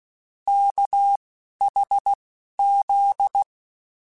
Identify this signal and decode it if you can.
Morse code, heard as a keyed tone, about 16 wpm, spaced as for 11 wpm, 800 Hz. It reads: KHZ
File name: Technical Subjects / Technical Subjects-mp3s / Sw Pwr Sup @10